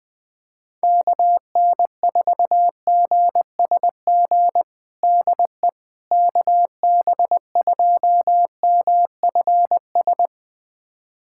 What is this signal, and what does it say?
Morse code, heard as a keyed tone, about 21 wpm, 700 Hz. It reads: KN4GSG DE KB2MFS
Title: USA Callsign DE Callsign - 6 Characters - 20wpm